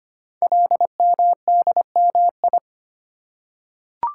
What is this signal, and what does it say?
Morse code, heard as a keyed tone, about 25 wpm, 700 Hz. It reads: LMDMI
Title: Code Groups - 5 Letters - 25wpm